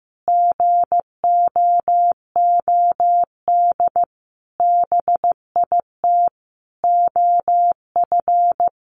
Morse code, heard as a keyed tone, about 15 wpm, 700 Hz. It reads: GOOD BIT OF
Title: Sentences from Top 900 Words 15wpm